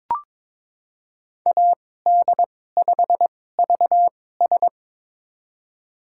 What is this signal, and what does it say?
Morse code, heard as a keyed tone, about 22 wpm, spaced as for 16 wpm, 700 Hz. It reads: AD5VS
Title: USA Call Signs - 2x Character Spacing - 22wpm